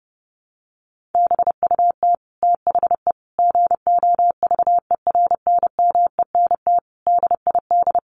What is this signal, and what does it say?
Morse code, heard as a keyed tone, about 30 wpm, 700 Hz. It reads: BUT THE GOVERNMENT DID
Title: Sentences from Top 1400 Words 30wpm